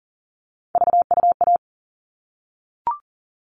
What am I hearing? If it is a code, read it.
Morse code, heard as a keyed tone, about 40 wpm, 700 Hz. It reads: VUA
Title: All Cut Numbers - 3-Digit 40wpm